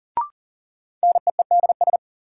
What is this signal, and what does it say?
Morse code, heard as a keyed tone, about 40 wpm, 700 Hz. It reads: NEEDS
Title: Sets of 3 Words - New Word Drill - 40wpm